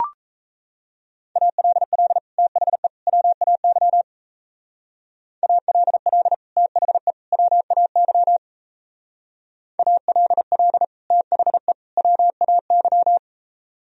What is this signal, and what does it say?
Morse code, heard as a keyed tone, about 32 wpm, 700 Hz. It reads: ALL THE WAY ALL THE WAY ALL THE WAY
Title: Speed-Racing - Sets of 3 Words 28wpm